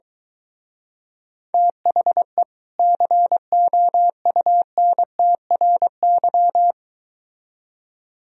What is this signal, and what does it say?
Morse code, heard as a keyed tone, about 22 wpm, 700 Hz. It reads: THE COUNTRY